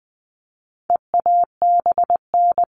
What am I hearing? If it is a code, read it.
Morse code, heard as a keyed tone, about 20 wpm, 700 Hz. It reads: EABN
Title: All Cut Numbers - 4-Digit 20wpm